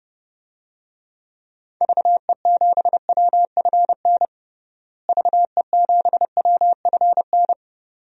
Morse code, heard as a keyed tone, about 30 wpm, 700 Hz. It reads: VE7WFN VE7WFN
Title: DXCC-100 Call Signs - 6 Characters - Repeated 2x - 30wpm